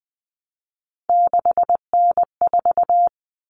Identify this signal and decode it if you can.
Morse code, heard as a keyed tone, about 20 wpm, 700 Hz. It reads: 6N4
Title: Common Cut Numbers - 3-Digit 20wpm